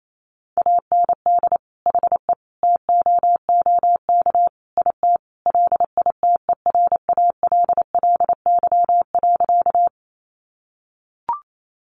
Morse code, heard as a keyed tone, about 28 wpm, 700 Hz. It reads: AND HE TOOK IT LITERALLY.
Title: Sentences from Top 1400 Words 28wpm